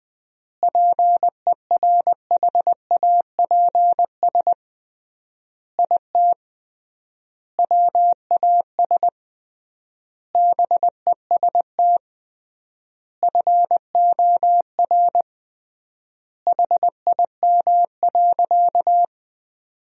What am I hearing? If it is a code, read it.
Morse code, heard as a keyed tone, about 20 wpm, 700 Hz. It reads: PERHAPS IT WAS BEST FOR HIM.
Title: 3x Word-Spacing - Sentences from Top 500 Words - 20wpm